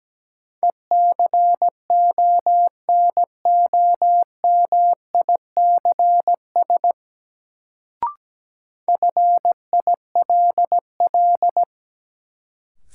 Morse code, heard as a keyed tone, about 17 wpm, 700 Hz. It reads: ECONOMICS FILL
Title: Sentences from Top 1000 Words - Review Part 3 - New Word Drill - 17wpm